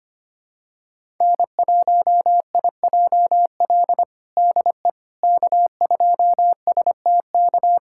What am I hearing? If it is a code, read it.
Morse code, heard as a keyed tone, about 25 wpm, 700 Hz. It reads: N1IJL DE K2STK
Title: USA Callsign DE Callsign - 5 Characters - 25wpm